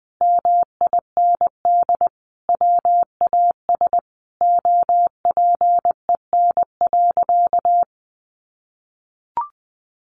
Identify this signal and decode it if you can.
Morse code, heard as a keyed tone, about 20 wpm, 700 Hz. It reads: MIND WAS OPEN.
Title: Sentences from Top 1200 Words 20wpm